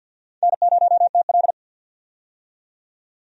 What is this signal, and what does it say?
Morse code, heard as a keyed tone, about 53 wpm, 700 Hz. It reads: N0TL
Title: USA Call Signs - 4 Characters - 50wpm